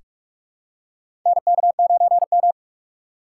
Morse code, heard as a keyed tone, about 44 wpm, 700 Hz. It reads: NK9M
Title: USA Call Signs - 4 Characters - 6x Character Spacing - 45wpm